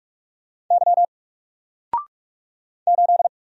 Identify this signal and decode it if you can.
Morse code, heard as a keyed tone, about 44 wpm, 700 Hz. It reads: Y 8
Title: Single Letter-Number Alt 45wpm